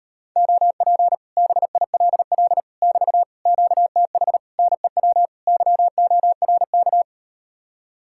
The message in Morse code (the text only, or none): OP BILL <BT> QTH NEW YORK